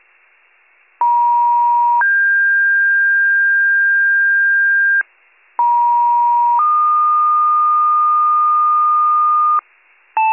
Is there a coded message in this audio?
Yes: two-tone page from 1.0 to 9.6 s, heard as a one-second tone, then a three-second tone, about 950 then 1600 Hz, about 950 then 1220 Hz, with noise.